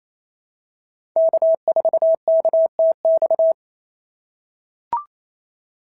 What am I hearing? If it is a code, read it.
Morse code, heard as a keyed tone, about 28 wpm, 690 Hz. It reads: K4KTX
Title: SOTA QSO Practice 28wpm